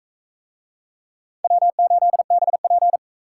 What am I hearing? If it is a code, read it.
Morse code, heard as a keyed tone, about 41 wpm, 700 Hz. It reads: W8BP